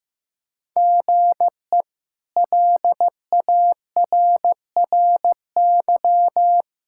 Morse code, heard as a keyed tone, about 15 wpm, 700 Hz. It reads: GE LARRY
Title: K1USN SST QSO Practice - 15wpm